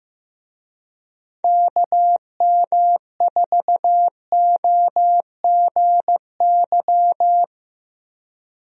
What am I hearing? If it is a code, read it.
Morse code, heard as a keyed tone, about 15 wpm, 700 Hz. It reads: KM4OGY
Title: USA Call Signs - 6 Characters - Repeated 3x - 15wpm